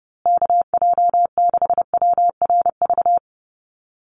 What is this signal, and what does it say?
Morse code, heard as a keyed tone, about 30 wpm, 700 Hz. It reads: KJ6WRV
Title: USA Call Signs Repeated 2x 30wpm